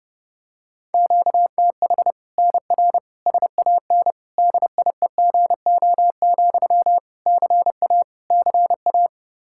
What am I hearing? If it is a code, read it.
Morse code, heard as a keyed tone, about 32 wpm, 700 Hz. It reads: QTH NR SAN DIEGO, CA CA
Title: Practice QSO - Exchange 1 - Beginning - 30wpm